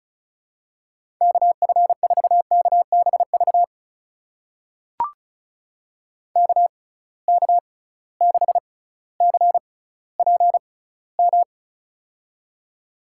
Morse code, heard as a keyed tone, about 34 wpm, spaced as for 18 wpm, 700 Hz. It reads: KF4KBV KK6CPM
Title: USA Call Signs - 6 Characters - 6x Character Spacing - 35wpm